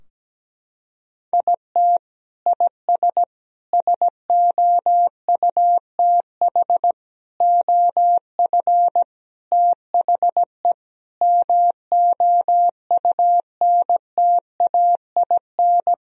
Morse code, heard as a keyed tone, about 17 wpm, 700 Hz. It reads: IT IS SOUTH OF THE MOUNTAIN
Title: Sentences from Top 1000 Words - Review Part 4 - 17wpm